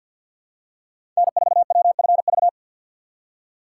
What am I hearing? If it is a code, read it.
Morse code, heard as a keyed tone, about 53 wpm, 700 Hz. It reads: N4WFV
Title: USA Call Signs - 4x Character Spacing - 50wpm